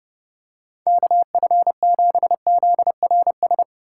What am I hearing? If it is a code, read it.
Morse code, heard as a keyed tone, about 30 wpm, 700 Hz. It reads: KF7ZRS
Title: USA Call Signs - 6 Characters - 3x Character Spacing - 30wpm